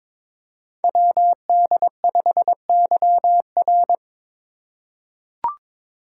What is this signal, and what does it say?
Morse code, heard as a keyed tone, about 22 wpm, 700 Hz. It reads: WD5YR
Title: Code Groups - 5 Letter:Numbers - 22wpm